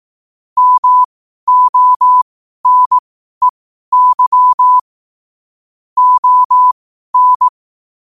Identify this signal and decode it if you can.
Morse code, heard as a keyed tone, about 18 wpm, spaced as for 12 wpm, 1000 Hz. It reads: MONEY ON
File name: qotdfiles / 12wpm